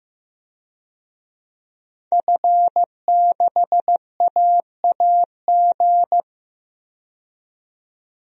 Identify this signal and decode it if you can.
Morse code, heard as a keyed tone, about 15 wpm, 700 Hz. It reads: F6AAG
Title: DXCC-100 Call Signs - 5 Characters - 15wpm